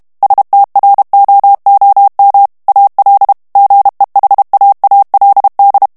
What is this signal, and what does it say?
Morse code, heard as a keyed tone, about 32 wpm, 800 Hz. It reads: STROOM AL GEHAALD
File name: HOLLANDAIS texte clair 2 vitesse 30 mots minute